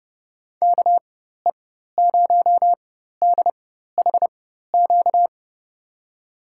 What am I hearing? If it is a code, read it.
Morse code, heard as a keyed tone, about 30 wpm, spaced as for 14 wpm, 700 Hz. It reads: KE0DHQ
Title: USA Call Signs - 6 Characters - 4x Character Spacing - 30wpm